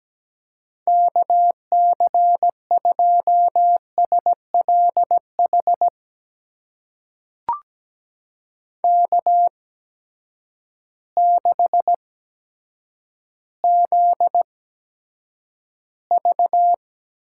Morse code, heard as a keyed tone, about 17 wpm, spaced as for 8 wpm, 700 Hz. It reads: KC2SLH K6ZV
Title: USA Call Signs - 8x Character Spacing - 17wpm